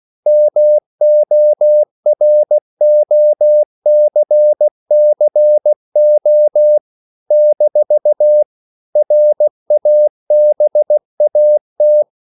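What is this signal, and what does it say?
Morse code, heard as a keyed tone, about 16 wpm, 600 Hz. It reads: MOROCCO - RABAT
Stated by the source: CW practice of national capitals at 16 words per minute, from ON6ZQ.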